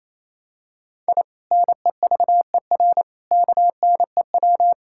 Morse code, heard as a keyed tone, about 27 wpm, 700 Hz. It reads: I NEVER KNEW
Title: Sentences from Top 1000 Words - Review Part 6 - 28wpm